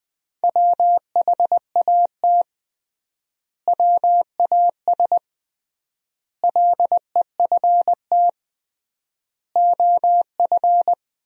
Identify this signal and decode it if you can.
Morse code, heard as a keyed tone, about 20 wpm, 700 Hz. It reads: WHAT WAS LEFT OF
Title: 3x Word-Spacing - Sentences from Top 300 Words - 20wpm